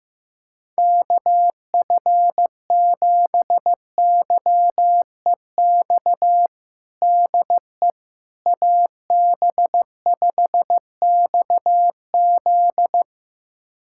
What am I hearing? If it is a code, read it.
Morse code, heard as a keyed tone, about 15 wpm, 700 Hz. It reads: KF7YEX DE AB5XZ
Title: USA Callsign DE Callsign 15wpm